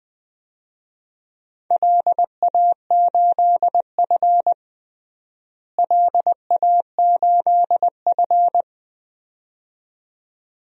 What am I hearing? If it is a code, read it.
Morse code, heard as a keyed tone, about 20 wpm, 700 Hz. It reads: LA8F LA8F
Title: DXCC-100 Call Signs - 4 Characters - Repeated 2x - 20wpm